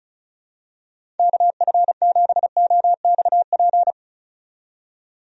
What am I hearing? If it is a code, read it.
Morse code, heard as a keyed tone, about 34 wpm, 700 Hz. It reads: KF7OXP
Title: USA Call Signs - 6 Characters - 35wpm